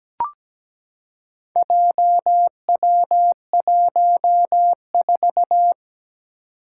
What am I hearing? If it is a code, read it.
Morse code, heard as a keyed tone, about 17 wpm, 700 Hz. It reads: JW14
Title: Four Challenging Letter-Number Combinations 17wpm